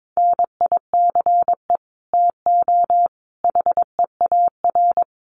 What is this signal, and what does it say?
Morse code, heard as a keyed tone, about 22 wpm, 700 Hz. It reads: NICE TO HEAR
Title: Common QSO Phrases 22wpm